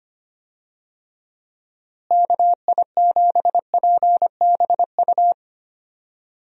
Morse code, heard as a keyed tone, about 25 wpm, 700 Hz. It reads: KI7PBU
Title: USA Call Signs - 6 Characters - Repeated 3x - 25wpm